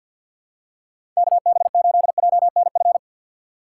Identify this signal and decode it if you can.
Morse code, heard as a keyed tone, about 51 wpm, 690 Hz. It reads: KB8JNF
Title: USA Call Signs - 2x Character Spacing - 50wpm